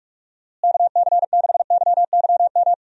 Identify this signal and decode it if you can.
Morse code, heard as a keyed tone, about 42 wpm, 700 Hz. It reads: KC6YYM
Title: USA Call Signs - 6 Characters - Repeated 2x - 45wpm